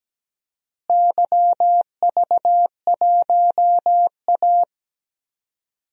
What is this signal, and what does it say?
Morse code, heard as a keyed tone, about 17 wpm, 700 Hz. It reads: YV1A
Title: DXCC-100 Call Signs - 4 Characters - Repeated 2x - 17wpm